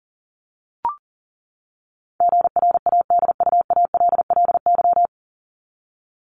Morse code, heard as a keyed tone, about 40 wpm, 700 Hz. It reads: GRADUALLY
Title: LY Suffix - Top 100 Words with Base Words 40wpm